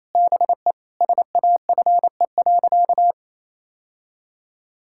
Morse code, heard as a keyed tone, about 28 wpm, 700 Hz. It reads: BE SAFE.
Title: Sentences from Top 1600 Words 28wpm